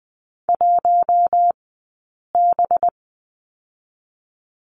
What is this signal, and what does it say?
Morse code, heard as a keyed tone, about 20 wpm, 700 Hz. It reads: 1 B
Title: 183 Lesson 30d - Letter b - Two Characters 20wpm